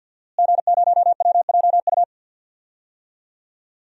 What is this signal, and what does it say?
Morse code, heard as a keyed tone, about 52 wpm, 700 Hz. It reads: G0WJU
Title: DXCC-100 Call Signs - 5 Characters - Repeated 3x - 50wpm